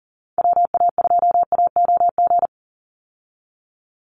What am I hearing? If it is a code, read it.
Morse code, heard as a keyed tone, about 40 wpm, 700 Hz. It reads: WA2AOG
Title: Call Signs Repeated 2x - 40wpm